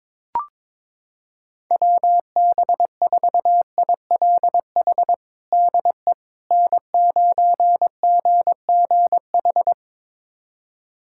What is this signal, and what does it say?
Morse code, heard as a keyed tone, about 22 wpm, 700 Hz. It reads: WB4ILH DE N9GGH
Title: USA Callsign DE Callsign 22wpm